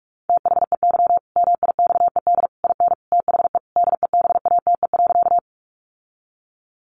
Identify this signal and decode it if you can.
Morse code, heard as a keyed tone, about 44 wpm, 700 Hz. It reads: THEY MIXED IN THE DEBATE.
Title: Sentences from Top 1500 Words 45wpm